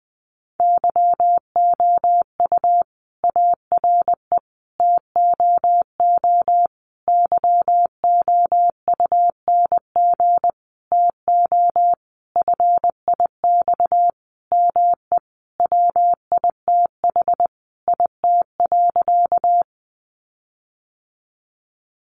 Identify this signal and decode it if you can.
Morse code, heard as a keyed tone, about 20 wpm, 700 Hz. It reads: YOU ARE TOO YOUNG TO FIX ME WITH IT.
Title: Sentences from Top 1400 Words 20wpm